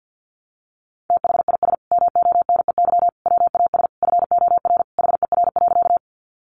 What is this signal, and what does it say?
Morse code, heard as a keyed tone, about 50 wpm, 700 Hz. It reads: THIS MONEY WAS FOR HER.
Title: Sentences from Top 400 Words 50wpm